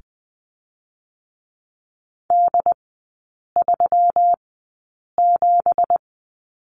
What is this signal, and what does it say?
Morse code, heard as a keyed tone, about 20 wpm, spaced as for 8 wpm, 700 Hz. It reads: D37
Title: 166 Lesson 27e - Number 7 - Three Characters 20wpm